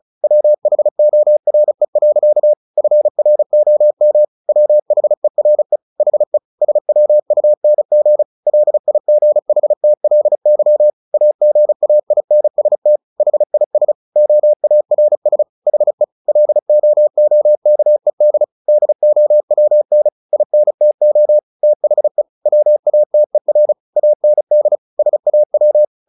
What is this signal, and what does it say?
Morse code, heard as a keyed tone, about 34 wpm, 600 Hz. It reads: WHORE. FROM WHERE HE SWUNG LIGHTLY AGAINST HIS OARS HE LOOKED DOWN INTO THE WATER AND SAW
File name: the old man and the sea / the old man and the sea 1 35wpm